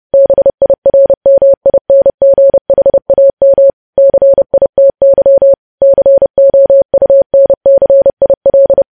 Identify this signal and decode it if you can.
Morse code, heard as a keyed tone, about 30 wpm, 550 Hz. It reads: BIRMINGHAM CITY COUNCIL
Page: news headlines in morse code at 30 wpm